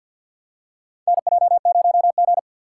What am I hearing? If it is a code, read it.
Morse code, heard as a keyed tone, about 51 wpm, 700 Hz. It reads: NJ0G
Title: USA Call Signs - 4 Characters - 6x Character Spacing - 50wpm